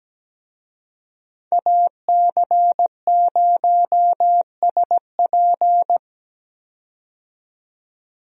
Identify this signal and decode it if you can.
Morse code, heard as a keyed tone, about 17 wpm, 700 Hz. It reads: AC0SP
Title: USA Call Signs - 5 Characters - 17wpm